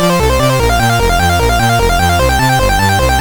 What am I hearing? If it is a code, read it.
There is no code.